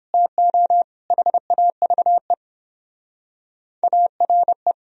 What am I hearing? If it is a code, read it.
Morse code, heard as a keyed tone, about 26 wpm, 700 Hz. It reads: TO HAVE ARE
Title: Speed-Racing - Sets of 4 Words 22wpm